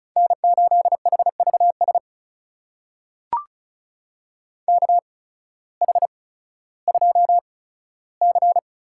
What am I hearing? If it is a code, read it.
Morse code, heard as a keyed tone, about 36 wpm, spaced as for 16 wpm, 700 Hz. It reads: N8HVS KH2C
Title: USA Call Signs - 5 Characters - 8x Character Spacing - 35wpm